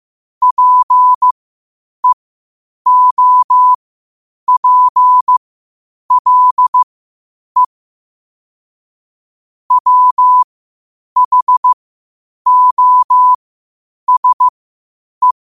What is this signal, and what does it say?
Morse code, heard as a keyed tone, about 15 wpm, spaced as for 8 wpm, 1000 Hz. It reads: PEOPLE WHOSE
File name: qotdfiles / 8wpm